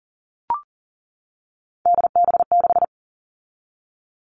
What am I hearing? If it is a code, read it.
Morse code, heard as a keyed tone, about 40 wpm, 700 Hz. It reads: DB6
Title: All Cut Numbers - 3-Digit 40wpm